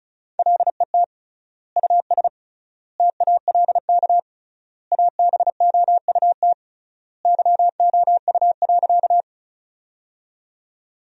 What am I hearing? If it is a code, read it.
Morse code, heard as a keyed tone, about 36 wpm, 700 Hz. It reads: LET US TALK ABOUT YOU.
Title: 3x Word-Spacing - Sentences from Top 400 Words - 35wpm